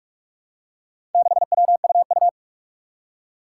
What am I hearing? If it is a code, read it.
Morse code, heard as a keyed tone, about 44 wpm, 700 Hz. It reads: 6WUU
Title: Four Challenging Letter-Number Combinations 45wpm